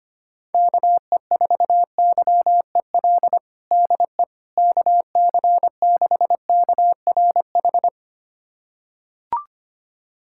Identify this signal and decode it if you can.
Morse code, heard as a keyed tone, about 25 wpm, 700 Hz. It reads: KE4YEL DE KC6KRH
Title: USA Callsign DE Callsign 25wpm